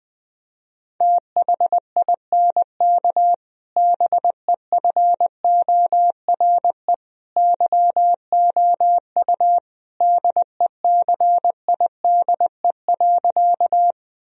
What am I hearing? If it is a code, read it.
Morse code, heard as a keyed tone, about 20 wpm, 700 Hz. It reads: THINK BEFORE YOU DECIDE.
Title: Sentences from Top 1000 Words - Review Part 3 - 20wpm